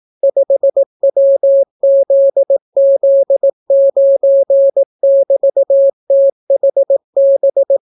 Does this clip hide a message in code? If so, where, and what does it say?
Yes: Morse code from 0.2 to 7.8 s, heard as a keyed tone, about 18 wpm, 550 Hz. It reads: 5WZZ9<BT>THB